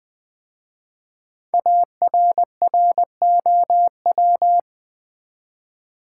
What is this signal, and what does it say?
Morse code, heard as a keyed tone, about 20 wpm, 700 Hz. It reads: ARROW
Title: 5 Letter Words - First 3 Letters Given - 20wpm